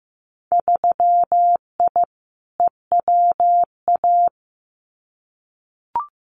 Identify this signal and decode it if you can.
Morse code, heard as a keyed tone, about 15 wpm, 700 Hz. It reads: 3I EWA
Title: Winter Field Day Exchanges Encore 15wpm